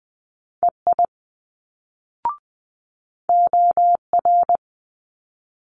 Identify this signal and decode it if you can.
Morse code, heard as a keyed tone, about 20 wpm, 700 Hz. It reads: EI OR
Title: Multi-letter Phonemes Quick Fire 20wpm